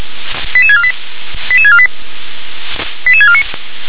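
Five-tone selective call (ZVEI), heard as quick runs of five tones, tones of about 70 ms. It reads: 88648 88648 88649
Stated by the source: Five-tone ZVEI selective call received on SDR.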